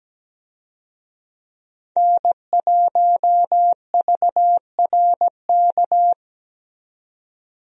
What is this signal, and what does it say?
Morse code, heard as a keyed tone, about 17 wpm, 700 Hz. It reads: N1VRK
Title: USA Call Signs - 5 Characters - 17wpm